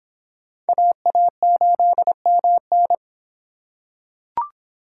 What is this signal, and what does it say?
Morse code, heard as a keyed tone, about 26 wpm, 700 Hz. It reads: AA8MN